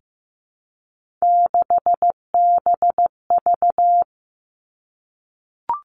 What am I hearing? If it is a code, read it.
Morse code, heard as a keyed tone, about 15 wpm, 700 Hz. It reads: 6BV
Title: Three Challenging Letter-Number Combinations 15wpm